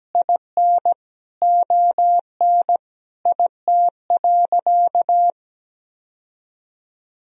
Morse code, heard as a keyed tone, about 17 wpm, 700 Hz. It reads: IN ON IT.
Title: Sentences from Top 1000 Words - Review Part 4 - 17wpm